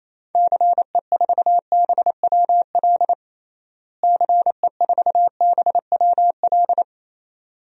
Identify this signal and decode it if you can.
Morse code, heard as a keyed tone, about 28 wpm, 700 Hz. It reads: CE4BWL CE4BWL
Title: DXCC-100 Call Signs - 6 Characters - Repeated 2x - 28wpm